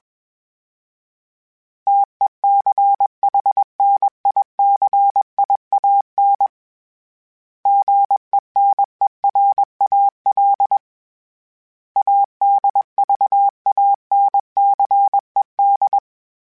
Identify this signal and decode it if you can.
Morse code, heard as a keyed tone, about 21 wpm, 800 Hz. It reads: TECHNICIAN GENERAL ADVANCED
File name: fromSMs / FromSMs mp3s / Classes@20